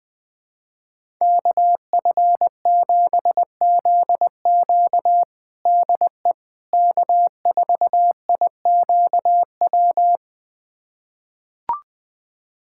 Morse code, heard as a keyed tone, about 21 wpm, 700 Hz. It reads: KF7ZQ DE K4IQW
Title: USA Callsign DE Callsign - 5 Characters - 20wpm